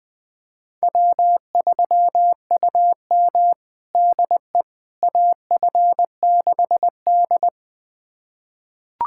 Morse code, heard as a keyed tone, about 21 wpm, 700 Hz. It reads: W3UM DE AF6D
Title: USA Callsign DE Callsign - 4 Characters - 20wpm